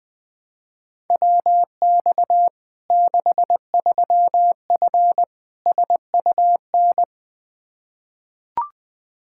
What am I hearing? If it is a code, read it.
Morse code, heard as a keyed tone, about 20 wpm, 700 Hz. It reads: WX 63F SUN
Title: Practice QSO - Exchange 2 - Info - 20wpm